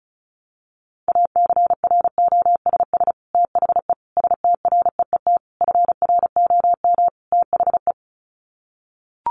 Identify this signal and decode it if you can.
Morse code, heard as a keyed tone, about 34 wpm, 700 Hz. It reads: ACROSS THE STREET FROM THE
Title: Sets of 5 Words Encore 35wpm